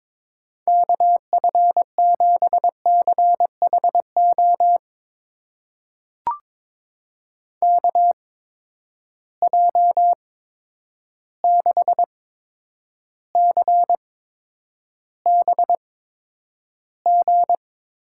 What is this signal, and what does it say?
Morse code, heard as a keyed tone, about 22 wpm, spaced as for 9 wpm, 700 Hz. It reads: KF7CHO KJ6CBG
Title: USA Call Signs - 6 Characters - 8x Character Spacing - 22wpm